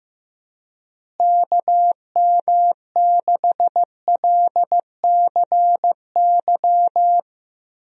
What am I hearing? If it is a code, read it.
Morse code, heard as a keyed tone, about 15 wpm, 700 Hz. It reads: KM6LCY
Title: USA Call Signs - 6 Characters - Repeated 2x - 15wpm